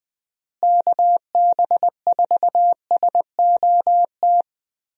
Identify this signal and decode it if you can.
Morse code, heard as a keyed tone, about 21 wpm, 700 Hz. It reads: KB4SOT
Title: USA Call Signs - 6 Characters - Repeated 2x - 20wpm